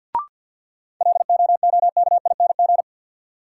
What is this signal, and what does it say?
Morse code, heard as a keyed tone, about 53 wpm, 700 Hz. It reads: LOOKING
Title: Sentences from Top 1000 Words - Review Part 6 - New Word Drill - 50wpm